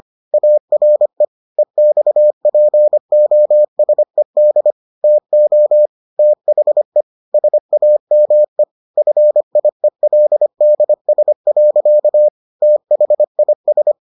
Morse code, heard as a keyed tone, about 25 wpm, 600 Hz. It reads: ARE EXPOSED TO THE SAME FIELDS. THIS